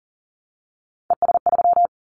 Morse code, heard as a keyed tone, about 42 wpm, 700 Hz. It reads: ES3
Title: Callsign Prefixes for Top 50 DXCC - 40wpm